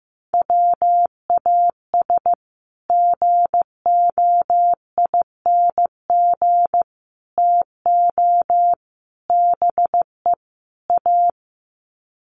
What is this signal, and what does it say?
Morse code, heard as a keyed tone, about 15 wpm, 700 Hz. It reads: WAS GOING TO BE A
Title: Sets of 5 Words 15wpm